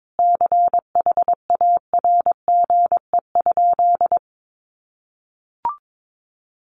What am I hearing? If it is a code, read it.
Morse code, heard as a keyed tone, about 22 wpm, 700 Hz. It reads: CHARGE?
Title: Sentences from Top 900 Words 22wpm